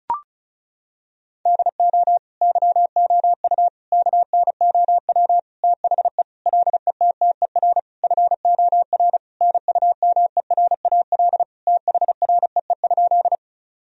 Morse code, heard as a keyed tone, about 34 wpm, 700 Hz. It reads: DO YOU KNOW THE LETTER FOR NUMERAL THREE?
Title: Sentences from Top 1000 Words - Review Part 2 - 35wpm